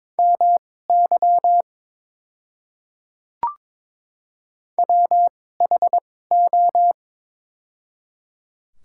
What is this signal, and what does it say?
Morse code, heard as a keyed tone, about 22 wpm, spaced as for 16 wpm, 700 Hz. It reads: MY WHO
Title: 2x Character Spacing - Top 200 Words - 22wpm